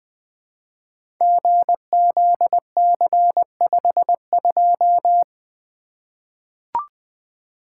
Morse code, heard as a keyed tone, about 20 wpm, 700 Hz. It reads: GZC52
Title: Code Groups - 5 Letter:Numbers - 20wpm